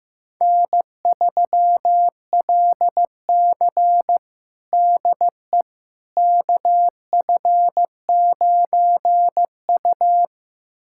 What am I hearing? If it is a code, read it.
Morse code, heard as a keyed tone, about 16 wpm, 700 Hz. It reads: N3LC DE KF9U
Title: USA Callsign DE Callsign - 4 Characters - 15wpm